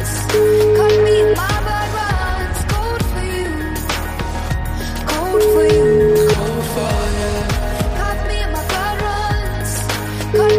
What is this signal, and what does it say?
Ringback tone (Europe, 425 Hz), with noise.